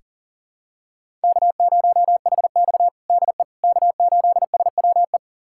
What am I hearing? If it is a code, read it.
Morse code, heard as a keyed tone, about 37 wpm, 700 Hz. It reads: K0HX DE K8SWE
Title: USA Callsign DE Callsign 40wpm